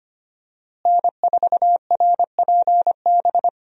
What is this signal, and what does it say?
Morse code, heard as a keyed tone, about 25 wpm, 700 Hz. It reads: N4RPB
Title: USA Call Signs - 5 Characters - Repeated 3x - 25wpm